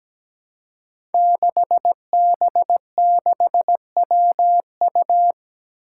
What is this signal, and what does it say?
Morse code, heard as a keyed tone, about 17 wpm, 700 Hz. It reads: 6B6WU
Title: Five Challenging Letter-Number Combinations 17wpm